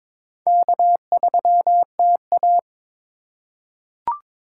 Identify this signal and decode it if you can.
Morse code, heard as a keyed tone, about 22 wpm, 700 Hz. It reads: K3TA
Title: USA Call Signs - 4 Characters - 2x Character Spacing - 22wpm